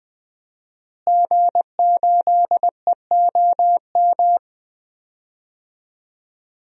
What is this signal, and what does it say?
Morse code, heard as a keyed tone, about 20 wpm, 700 Hz. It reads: G8EOM
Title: DXCC-100 Call Signs - 5 Characters - Repeated 2x - 20wpm